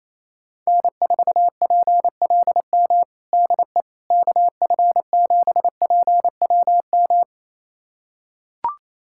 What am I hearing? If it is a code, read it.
Morse code, heard as a keyed tone, about 28 wpm, 700 Hz. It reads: N4PLM DE KF7PWM
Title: USA Callsign DE Callsign 28wpm